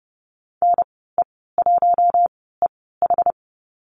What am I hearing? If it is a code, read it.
Morse code, heard as a keyed tone, about 30 wpm, spaced as for 17 wpm, 700 Hz. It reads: NE1EH
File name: Lesson 14g - Number 2 - Call Signs 30wpm